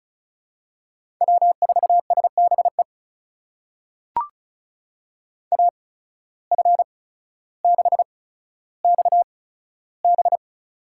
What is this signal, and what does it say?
Morse code, heard as a keyed tone, about 36 wpm, spaced as for 15 wpm, 700 Hz. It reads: W4SBE AF6XB
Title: USA Call Signs - 5 Characters - 8x Character Spacing - 35wpm